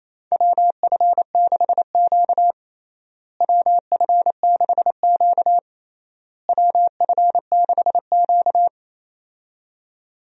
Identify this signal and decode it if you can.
Morse code, heard as a keyed tone, about 27 wpm, 700 Hz. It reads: WF6Q WF6Q WF6Q
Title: USA Call Signs - 4 Characters - Repeated 3x - 28wpm